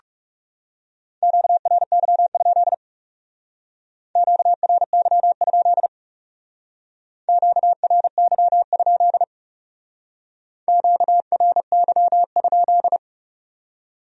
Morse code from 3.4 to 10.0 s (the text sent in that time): QRY? QRY?